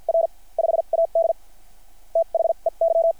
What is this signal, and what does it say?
Morse code, heard as a keyed tone, about 50 wpm, spaced as for 39 wpm, 660 Hz. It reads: R 5AD THEY
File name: archived / CRSnet